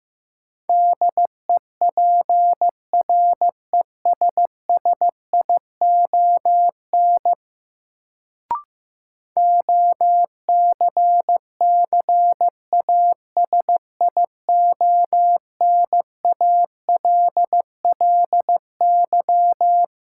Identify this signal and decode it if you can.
Morse code, heard as a keyed tone, about 15 wpm, 700 Hz. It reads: DEPRESSION OCCASIONALLY
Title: Sentences from Top 1300 Words - New Word Drill - 15wpm